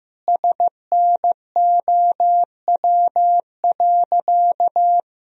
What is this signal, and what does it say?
Morse code, heard as a keyed tone, about 15 wpm, 700 Hz. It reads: SNOW.
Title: Sentences from Top 1000 Words - Review Part 3 - 15wpm